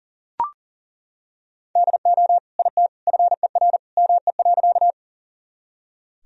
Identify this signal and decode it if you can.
Morse code, heard as a keyed tone, about 40 wpm, 700 Hz. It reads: DO IT FER ME.
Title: Sentences from Top 200 Words and Abbreviations 40wpm